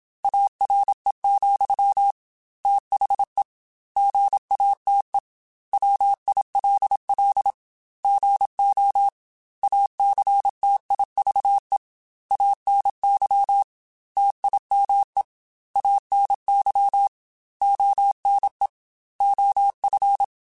Morse code, heard as a keyed tone, about 27 wpm, 800 Hz. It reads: ARE, THE GATE WILL GO ACTIVE ANY TIME ANY ONE OF